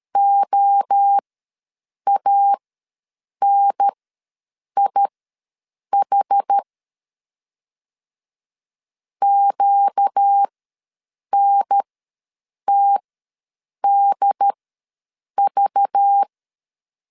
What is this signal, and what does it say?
Morse code, heard as a keyed tone, about 13 wpm, spaced as for 7 wpm, 800 Hz. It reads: OANIH QNTDV
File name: LEZIONI / 5 LBVC / 5 LBV ws 12wpm 3015